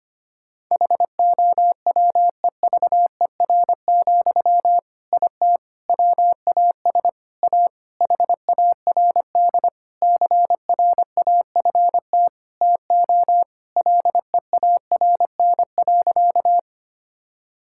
Morse code, heard as a keyed tone, about 25 wpm, 700 Hz. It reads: HOWEVER, IT WAS A HARD CRAFT TO LEARN.
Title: Sentences from Top 1000 Words - Review Part 6 - 25wpm